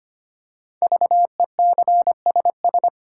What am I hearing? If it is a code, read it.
Morse code, heard as a keyed tone, about 25 wpm, 700 Hz. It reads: VECSS
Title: Code Groups - 5 Letters - 25wpm